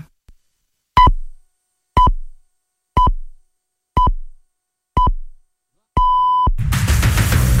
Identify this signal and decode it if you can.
Radio time signal (time pips). It is heard from 1.0 to 6.5 s.